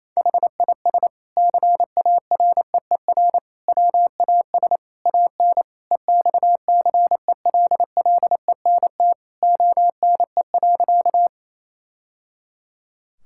Morse code, heard as a keyed tone, about 27 wpm, 700 Hz. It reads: HIS CAREER WAS AN EXCELLENT ONE.
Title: Sentences from Top 1000 Words - Review Part 6 - 28wpm